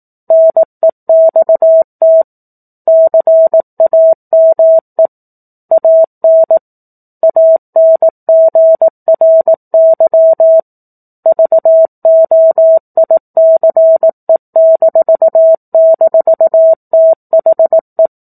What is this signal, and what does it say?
Morse code, heard as a keyed tone, about 18 wpm, 650 Hz. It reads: NEXT CAME AN ANGRY VOICE--THE
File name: 04-Chapter-04 18 WPM